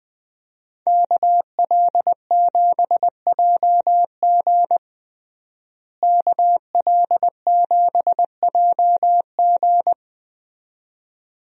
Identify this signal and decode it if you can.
Morse code, heard as a keyed tone, about 20 wpm, 700 Hz. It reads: KL7JG KL7JG
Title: DXCC-100 Call Signs - 5 Characters - Repeated 2x - 20wpm